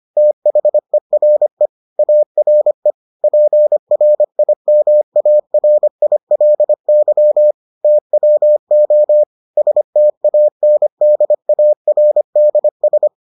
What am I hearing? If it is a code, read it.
Morse code, heard as a keyed tone, about 25 wpm, 600 Hz. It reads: THERE ARE PRIMARILY TWO STANDARDS